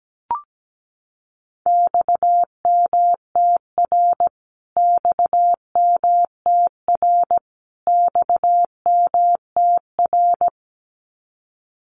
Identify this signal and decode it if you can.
Morse code, heard as a keyed tone, about 17 wpm, 700 Hz. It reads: XMTR XMTR XMTR
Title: Abbreviations Repeated 3x - 17wpm